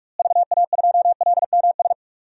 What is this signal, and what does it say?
Morse code, heard as a keyed tone, about 46 wpm, 700 Hz. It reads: VA2LMS
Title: DXCC-100 Call Signs - 6 Characters - Repeated 2x - 45wpm